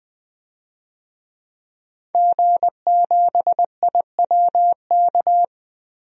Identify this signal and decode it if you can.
Morse code, heard as a keyed tone, about 20 wpm, 700 Hz. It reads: G7IWK
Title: DXCC-100 Call Signs - 5 Characters - 20wpm